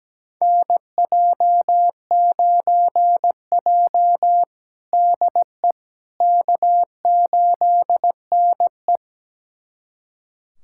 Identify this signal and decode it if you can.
Morse code, heard as a keyed tone, about 17 wpm, 700 Hz. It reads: NJ9J DE K8NE
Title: USA Callsign DE Callsign - 4 Characters - 17wpm